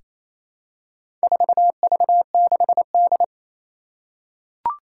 Morse code, heard as a keyed tone, about 27 wpm, 700 Hz. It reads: 4V6D
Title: Four Challenging Letter-Number Combinations 28wpm